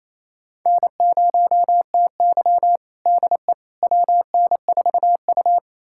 Morse code, heard as a keyed tone, about 27 wpm, 700 Hz. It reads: N0TY DE WN4U
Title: USA Callsign DE Callsign - 4 Characters - 28wpm